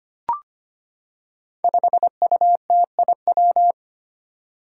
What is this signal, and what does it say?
Morse code, heard as a keyed tone, about 25 wpm, 700 Hz. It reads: 5UTIW
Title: Five Challenging Letter-Number Combinations 25wpm